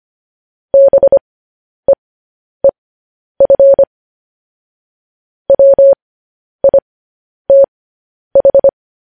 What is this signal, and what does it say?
Morse code, heard as a keyed tone, about 25 wpm, spaced as for 10 wpm, 550 Hz. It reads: BEEF WITH